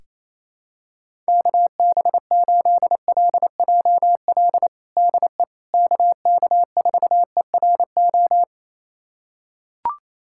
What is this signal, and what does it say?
Morse code, heard as a keyed tone, about 27 wpm, 700 Hz. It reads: KB8LJL DE KK4ERO
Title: USA Callsign DE Callsign 28wpm